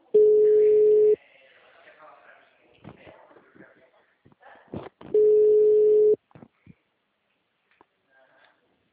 Ringback tone (Europe, 425 Hz).